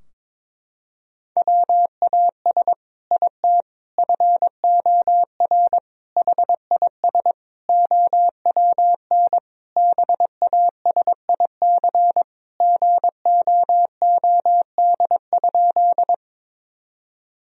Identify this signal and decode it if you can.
Morse code, heard as a keyed tone, about 22 wpm, 700 Hz. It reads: WAS IT FOR HIS OWN BASIC GOOD?
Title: Sentences from Top 1000 Words - Review Part 3 - 22wpm